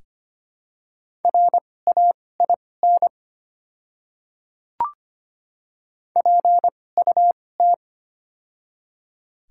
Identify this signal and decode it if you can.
Morse code, heard as a keyed tone, about 25 wpm, spaced as for 18 wpm, 700 Hz. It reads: RAIN PUT
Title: 2x Character Spacing - Top 300 Words - 25wpm